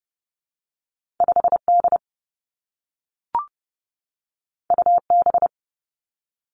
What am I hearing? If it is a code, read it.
Morse code, heard as a keyed tone, about 30 wpm, 700 Hz. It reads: 5 D U B
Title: Two Challenging Letter-Number Combinations 30wpm